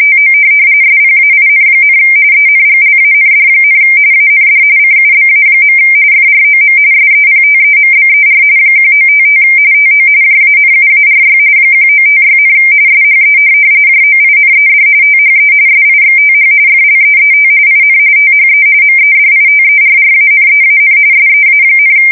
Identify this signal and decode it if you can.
RTTY, about 45.45 baud. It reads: TO SPAIN, ENGLAND, AND CUBA. FROM SEPTEMBER 1955 TO JANUARY 1956, SHE JOINED THE SIXTH FLEET IN THE MEDITERRANEAN SEA,
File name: fsk170 / USS-New-Jersey